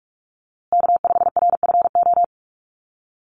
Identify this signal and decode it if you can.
Morse code, heard as a keyed tone, about 44 wpm, 700 Hz. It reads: K5RFO
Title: USA Call Signs Repeated 2x 45wpm